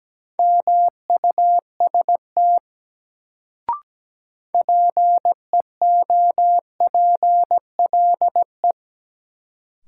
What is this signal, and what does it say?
Morse code, heard as a keyed tone, about 17 wpm, 700 Hz. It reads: MUST PEOPLE
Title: Sets of 2 Words - New Word Drill - 17wpm